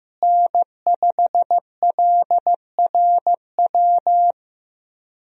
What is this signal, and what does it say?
Morse code, heard as a keyed tone, about 15 wpm, 700 Hz. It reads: N5LRW
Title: USA Call Signs - 5 Characters - Repeated 2x - 15wpm